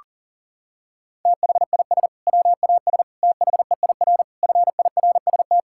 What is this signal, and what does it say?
Morse code, heard as a keyed tone, about 40 wpm, 700 Hz. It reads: THIS WAS THEIR FIRST
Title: Sentences from Top 1600 Words 40wpm